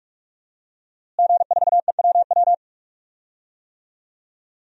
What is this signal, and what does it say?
Morse code, heard as a keyed tone, about 46 wpm, 700 Hz. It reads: G4EWW
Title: DXCC-100 Call Signs - 5 Characters - Repeated 2x - 45wpm